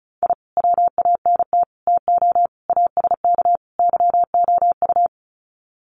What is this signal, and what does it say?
Morse code, heard as a keyed tone, about 34 wpm, 700 Hz. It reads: I WANT TO ASK YOU
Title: Sets of 5 Words 35wpm